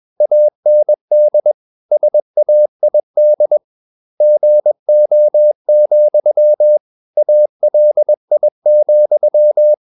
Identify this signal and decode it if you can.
Morse code, heard as a keyed tone, about 21 wpm, 600 Hz. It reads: AND SAID GO, ALI,
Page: Romanen Never af Ken Follet på engelsk.